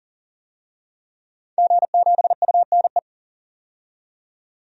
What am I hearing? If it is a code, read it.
Morse code, heard as a keyed tone, about 40 wpm, 700 Hz. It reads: G7UNE
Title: DXCC-100 Call Signs - 5 Characters - 40wpm